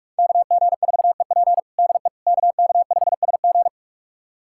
Morse code, heard as a keyed tone, about 44 wpm, 700 Hz. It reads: KG4EP DE KK5SG
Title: USA Callsign DE Callsign - 5 Characters - 45wpm